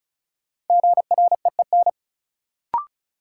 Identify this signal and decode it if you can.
Morse code, heard as a keyed tone, about 34 wpm, 700 Hz. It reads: GREEN
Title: Sentences from Top 500 Words - New Word Drill - 35wpm